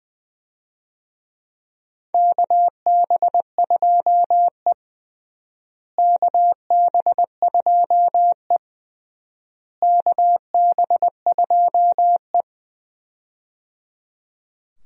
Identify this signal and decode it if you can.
Morse code, heard as a keyed tone, about 20 wpm, 700 Hz. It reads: KB2E KB2E KB2E
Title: USA Call Signs - 4 Characters - Repeated 3x - 20wpm